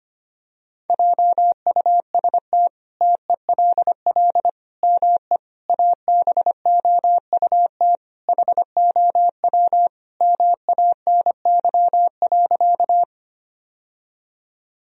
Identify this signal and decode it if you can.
Morse code, heard as a keyed tone, about 25 wpm, 700 Hz. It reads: JUST TELL ME ABOUT HOW MANY.
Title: Sentences from Top 1000 Words - Review Part 1 - 25wpm